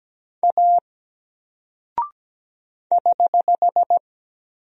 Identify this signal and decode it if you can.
Morse code, heard as a keyed tone, about 17 wpm, 700 Hz. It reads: A <HH>
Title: Single Character (Letter:Number:Punctuation:Prosign) 17wpm